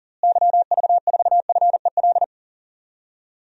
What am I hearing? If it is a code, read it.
Morse code, heard as a keyed tone, about 37 wpm, 700 Hz. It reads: YV4FEL
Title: DXCC-100 Call Signs - 6 Characters - Repeated 3x - 40wpm